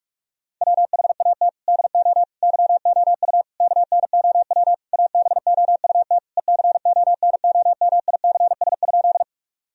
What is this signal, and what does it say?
Morse code, heard as a keyed tone, about 46 wpm, 700 Hz. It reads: WHAT DO YOU KNOW ABOUT ECONOMICS?
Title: Sentences from Top 1000 Words - Review Part 3 - 45wpm